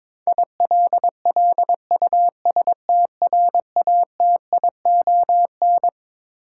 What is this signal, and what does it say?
Morse code, heard as a keyed tone, about 22 wpm, 700 Hz. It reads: ILLUSTRATION
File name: Twelve Letter Words - 22wpm